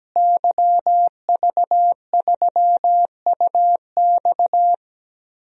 Morse code, heard as a keyed tone, about 17 wpm, 700 Hz. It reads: YV3UX
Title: DXCC-100 Call Signs - 5 Characters - Repeated 3x - 17wpm